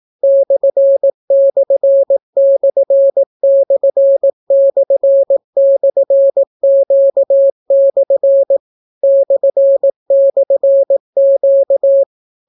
Morse code, heard as a keyed tone, about 18 wpm, 550 Hz. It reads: //////Q/ //Q